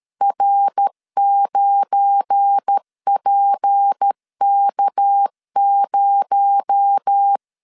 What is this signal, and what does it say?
Morse code, heard as a keyed tone, about 13 wpm, 800 Hz. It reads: R9PK0
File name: LEZIONI / 7 GF9Z / 7 GF9 12wpm 2319